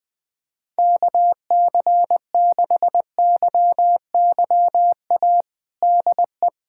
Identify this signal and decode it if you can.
Morse code, heard as a keyed tone, about 20 wpm, 700 Hz. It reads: KC6YYA DE
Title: USA Callsign DE Callsign - 6 Characters - 20wpm